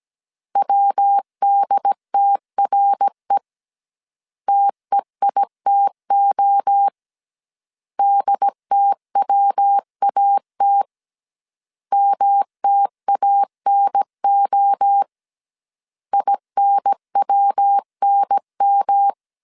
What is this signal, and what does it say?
Morse code, heard as a keyed tone, about 17 wpm, 800 Hz. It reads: WDTRE TEITO DTWAT MTANO INWNM